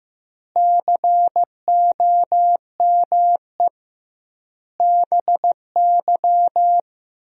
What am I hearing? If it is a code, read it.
Morse code, heard as a keyed tone, about 15 wpm, 700 Hz. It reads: COME BY
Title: 2x Word-Spacing - Sentences from Top 100 Words - 15wpm